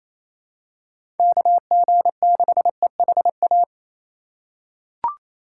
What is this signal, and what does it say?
Morse code, heard as a keyed tone, about 28 wpm, 700 Hz. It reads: KG6EHA
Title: USA Call Signs - 6 Characters - 6x Character Spacing - 28wpm